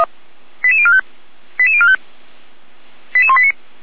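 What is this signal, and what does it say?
Five-tone selective call (ZVEI), heard as quick runs of five tones, tones of about 80 ms. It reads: 88045 88045 88189
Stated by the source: ZVEI five-tone selective call received on an SDR by the Feuerwehr Orlamünde fire brigade.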